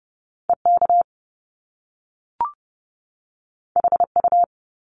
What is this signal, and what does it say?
Morse code, heard as a keyed tone, about 30 wpm, 700 Hz. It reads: EK HU
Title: Top Two Letter Combinations in Words 30wpm